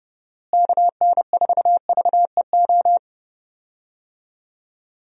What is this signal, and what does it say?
Morse code, heard as a keyed tone, about 32 wpm, 700 Hz. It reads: KN4VEO
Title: USA Call Signs - 6 Characters - Repeated 3x - 30wpm